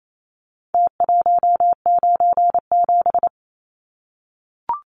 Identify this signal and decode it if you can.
Morse code, heard as a keyed tone, about 28 wpm, 700 Hz. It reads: T197
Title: Numbers - Contest Style - 4-Digit 28wpm